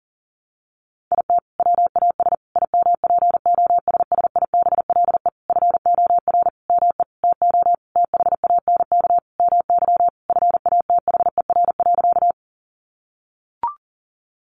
Morse code, heard as a keyed tone, about 40 wpm, 700 Hz. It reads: IT WAS IMPOSSIBLE FOR ME TO THANK MY FATHER.
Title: Sentences from Top 1100 Words 40wpm